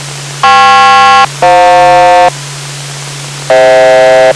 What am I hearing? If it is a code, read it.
Aircraft SELCAL, heard as two pairs of tones. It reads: KP-FJ